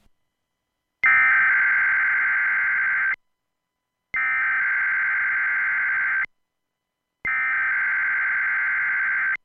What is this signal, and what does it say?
Emergency Alert System (SAME) tones (header bursts), heard as harsh buzzing bursts. It is heard from 1.1 to 8.4 s.